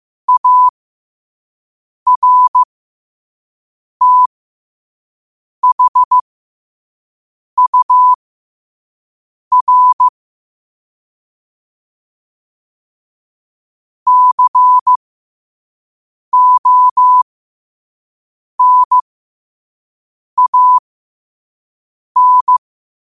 Morse code, heard as a keyed tone, about 15 wpm, spaced as for 5 wpm, 1000 Hz. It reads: ARTHUR CONAN
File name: qotdfiles / 5wpm